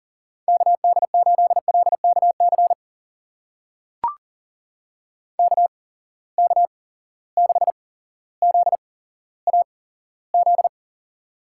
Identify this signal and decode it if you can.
Morse code, heard as a keyed tone, about 39 wpm, spaced as for 17 wpm, 700 Hz. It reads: KD8LKC KK6ZAZ
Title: USA Call Signs - 6 Characters - 8x Character Spacing - 40wpm